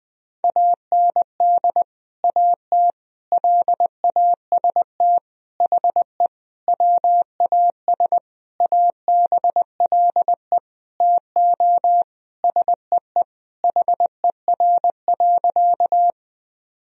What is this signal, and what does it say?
Morse code, heard as a keyed tone, about 20 wpm, 700 Hz. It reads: AND AT LAST HE WAS ABLE TO SEE HER.
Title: Sentences from Top 1000 Words - Review Part 2 - 20wpm